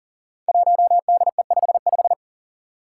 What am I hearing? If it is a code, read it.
Morse code, heard as a keyed tone, about 40 wpm, 700 Hz. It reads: 1DE55
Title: Five Challenging Letter-Number Combinations 40wpm